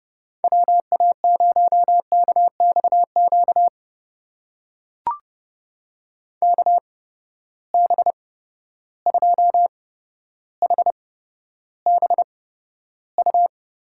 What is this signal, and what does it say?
Morse code, heard as a keyed tone, about 31 wpm, spaced as for 13 wpm, 700 Hz. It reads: WA0KXQ KB2HBU
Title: USA Call Signs - 6 Characters - 8x Character Spacing - 30wpm